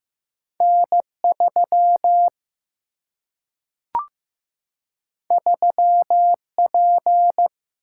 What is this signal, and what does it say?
Morse code, heard as a keyed tone, about 15 wpm, 700 Hz. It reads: N 3 3 P
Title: Single-Letter Single-Number 15wpm